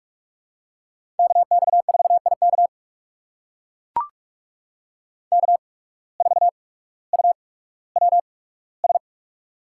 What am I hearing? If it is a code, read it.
Morse code, heard as a keyed tone, about 44 wpm, spaced as for 19 wpm, 700 Hz. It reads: KX4IK K4UWS
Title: USA Call Signs - 5 Characters - 8x Character Spacing - 45wpm